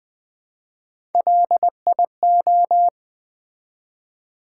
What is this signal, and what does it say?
Morse code, heard as a keyed tone, about 20 wpm, 700 Hz. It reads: LIO
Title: Build A Word - 1 Letter At A Time - 4-Letter Words - 20wpm